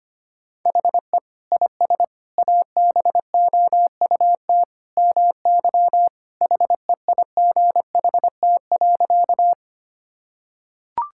Morse code, heard as a keyed tone, about 25 wpm, 700 Hz. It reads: HE IS ABOUT MY HEIGHT.
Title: Sentences from Top 1600 Words 25wpm